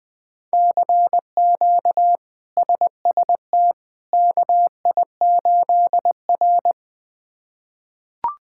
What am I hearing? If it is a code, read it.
Morse code, heard as a keyed tone, about 21 wpm, 700 Hz. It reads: CQ SST KI8R
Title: K1USN SST QSO Practice - 20wpm